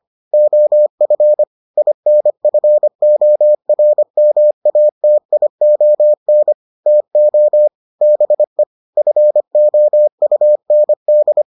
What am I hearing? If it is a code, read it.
Morse code, heard as a keyed tone, about 25 wpm, 600 Hz. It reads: OF INFORMATION TO BE FOUND